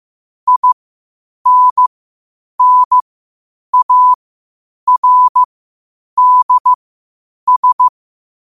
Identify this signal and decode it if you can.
Morse code, heard as a keyed tone, about 15 wpm, spaced as for 8 wpm, 1000 Hz. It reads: INNARDS
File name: qotdfiles / 8wpm